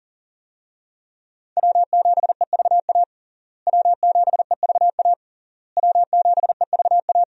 Morse code, heard as a keyed tone, about 40 wpm, 700 Hz. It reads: W7EVA W7EVA W7EVA
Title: USA Call Signs - 5 Characters - Repeated 3x - 40wpm